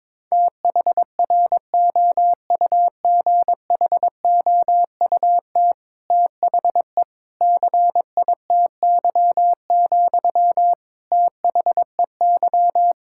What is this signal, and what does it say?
Morse code, heard as a keyed tone, about 22 wpm, 700 Hz. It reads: THROUGHOUT THE CITY, THEY
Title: Sentences from Top 1000 Words - Review Part 5 - 22wpm